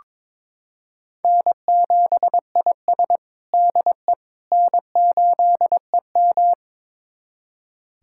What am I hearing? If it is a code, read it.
Morse code, heard as a keyed tone, about 22 wpm, 700 Hz. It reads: N7IS DE N8EM
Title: USA Callsign DE Callsign - 4 Characters - 22wpm